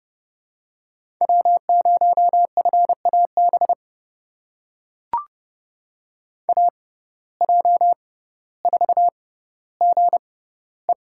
Morse code, heard as a keyed tone, about 30 wpm, spaced as for 15 wpm, 700 Hz. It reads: W0FAB AJ4GE
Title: USA Call Signs - 5 Characters - 6x Character Spacing - 30wpm